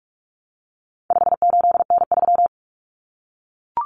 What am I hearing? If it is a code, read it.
Morse code, heard as a keyed tone, about 44 wpm, 700 Hz. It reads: 58N3
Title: Common Cut Numbers - 4-Digit 45wpm